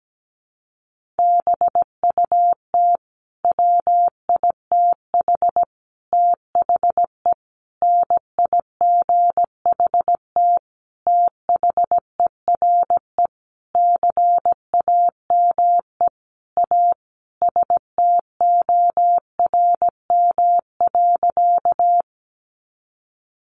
Morse code, heard as a keyed tone, about 17 wpm, 700 Hz. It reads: BUT WITH THE NIGHT THERE CAME A STORM.
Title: Sentences from Top 1500 Words 17wpm